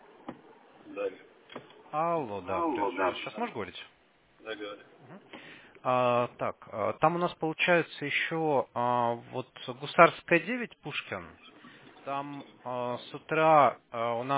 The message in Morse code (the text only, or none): none